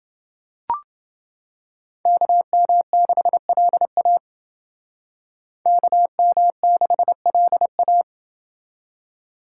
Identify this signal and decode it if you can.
Morse code, heard as a keyed tone, about 28 wpm, 700 Hz. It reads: KM6LA KM6LA